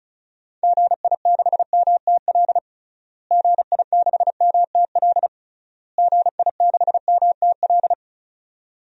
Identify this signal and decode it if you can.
Morse code, heard as a keyed tone, about 36 wpm, 700 Hz. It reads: GI6MTL GI6MTL GI6MTL
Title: DXCC-100 Call Signs - 6 Characters - Repeated 3x - 35wpm